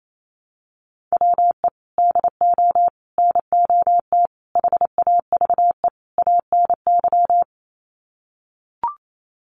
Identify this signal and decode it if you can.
Morse code, heard as a keyed tone, about 28 wpm, 700 Hz. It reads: WE DO NOT HAVE ANY
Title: Sets of 5 Words 28wpm